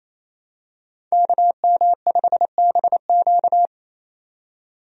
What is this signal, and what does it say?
Morse code, heard as a keyed tone, about 27 wpm, 700 Hz. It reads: KM5BQ
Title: USA Call Signs - 5 Characters - 28wpm